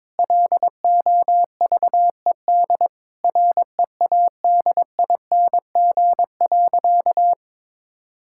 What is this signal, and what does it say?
Morse code, heard as a keyed tone, about 22 wpm, 700 Hz. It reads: LOVED READING.
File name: practice / 22wpm / Sentences with 8-Word Phrases - 22wpm